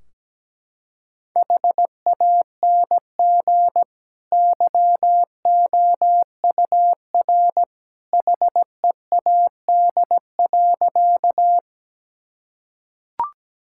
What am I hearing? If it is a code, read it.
Morse code, heard as a keyed tone, about 17 wpm, 700 Hz. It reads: HANG YOUR HEAD.
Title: Sentences from Top 1600 Words 17wpm